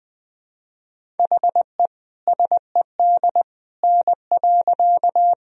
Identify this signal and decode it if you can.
Morse code, heard as a keyed tone, about 21 wpm, 700 Hz. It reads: HE SED N.
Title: Sentences from Top 100 Words and Abbreviations 20wpm